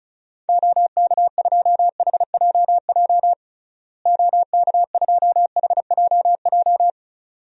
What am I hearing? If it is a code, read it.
Morse code, heard as a keyed tone, about 36 wpm, 700 Hz. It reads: OK2HJJ OK2HJJ
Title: DXCC-100 Call Signs - 6 Characters - Repeated 2x - 35wpm